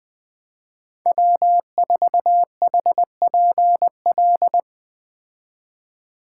Morse code, heard as a keyed tone, about 20 wpm, 700 Hz. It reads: W4HPL
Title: Speed Racing - USA 1x3 Call Signs 17wpm